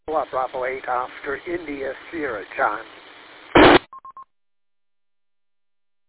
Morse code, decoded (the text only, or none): none